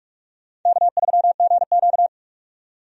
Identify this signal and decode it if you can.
Morse code, heard as a keyed tone, about 46 wpm, 700 Hz. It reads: K3GQ
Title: USA Call Signs - 4 Characters - 45wpm